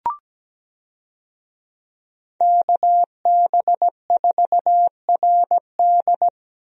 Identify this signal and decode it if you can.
Morse code, heard as a keyed tone, about 17 wpm, 700 Hz. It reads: KB4RD
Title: USA Call Signs - 5 Characters - Repeated 2x - 17wpm